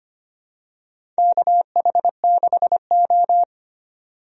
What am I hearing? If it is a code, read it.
Morse code, heard as a keyed tone, about 25 wpm, 700 Hz. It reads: KH6O
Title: DXCC-100 Call Signs - 4 Characters - 25wpm